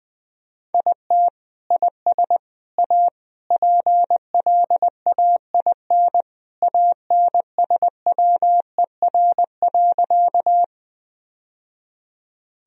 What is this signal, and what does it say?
Morse code, heard as a keyed tone, about 20 wpm, 700 Hz. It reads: IT IS A PLAIN ANSWER.
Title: Sentences from Top 1000 Words - Review Part 2 - 20wpm